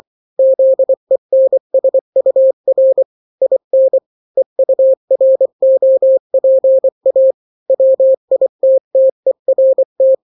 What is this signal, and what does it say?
Morse code, heard as a keyed tone, about 24 wpm, 530 Hz. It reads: ZENSUR IN EUROPA WITTERT
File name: daily / 23wpm